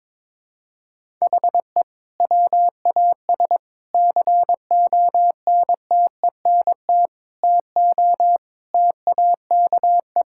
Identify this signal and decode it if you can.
Morse code, heard as a keyed tone, about 22 wpm, 700 Hz. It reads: HE WAS CONTENT TO TAKE
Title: Sentences from Top 1000 Words - Review Part 6 - 22wpm